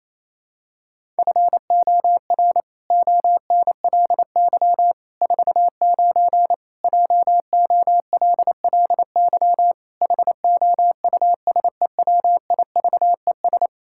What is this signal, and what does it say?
Morse code, heard as a keyed tone, about 28 wpm, 700 Hz. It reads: FOR ONLY 49 JOLLY HOUSEWIVES
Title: Pangram Challenge v2 - 28wpm